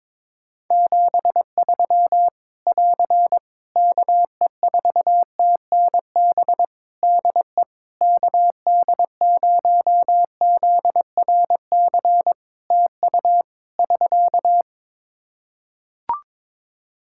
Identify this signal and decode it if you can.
Morse code, heard as a keyed tone, about 22 wpm, 700 Hz. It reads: 73 <AR> KE4TNB DE KD0ZRC TU <SK>
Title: K1USN SST QSO Practice 22wpm